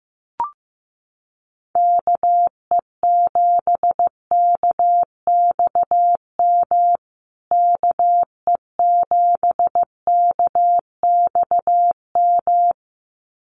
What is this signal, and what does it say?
Morse code, heard as a keyed tone, about 15 wpm, 700 Hz. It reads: KE7KXM KE7KXM
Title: USA Call Signs Repeated 2x 15wpm